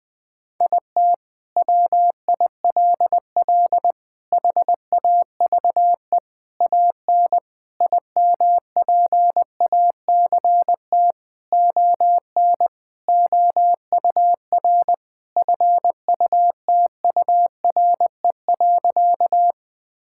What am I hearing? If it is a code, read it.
Morse code, heard as a keyed tone, about 20 wpm, 700 Hz. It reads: IT WILL HAVE AN IMPACT ON OUR FUTURE.
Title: Sentences from Top 1000 Words - Review Part 6 - 20wpm